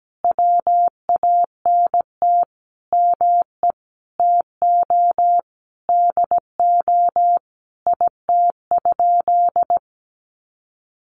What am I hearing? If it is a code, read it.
Morse code, heard as a keyed tone, about 17 wpm, 700 Hz. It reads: WANT ME TO DO IT?
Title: Sentences from Top 200 Words 17wpm